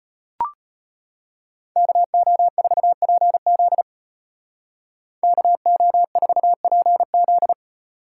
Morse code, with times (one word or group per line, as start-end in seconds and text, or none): 1.8-3.8 KO4PZ
5.2-7.5 KO4PZ